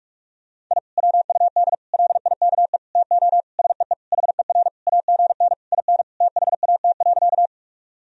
Morse code, heard as a keyed tone, about 41 wpm, 700 Hz. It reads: I WUD LIKE TO SEE HER AGN IN THAT.
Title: Sentences from Top 200 Words and Abbreviations 45wpm